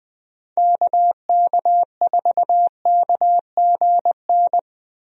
Morse code, heard as a keyed tone, about 20 wpm, 700 Hz. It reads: KK4KGN
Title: USA Call Signs - 6 Characters - 20wpm